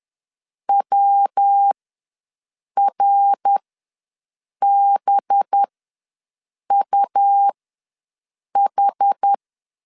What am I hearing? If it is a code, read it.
Morse code, heard as a keyed tone, about 10 wpm, spaced as for 6 wpm, 800 Hz. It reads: WRBUH
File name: LEZIONI / 5 LBVC / 5 LB ws 10wpm 3616